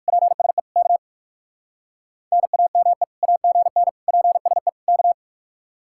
Morse code, heard as a keyed tone, about 68 wpm, spaced as for 57 wpm, 700 Hz. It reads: PSE K NAME AGN PSE K